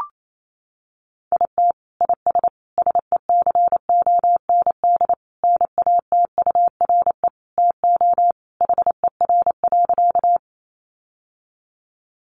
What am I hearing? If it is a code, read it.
Morse code, heard as a keyed tone, about 28 wpm, 700 Hz. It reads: IT IS SECOND NATURE TO HER.
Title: Sentences from Top 600 Words 28wpm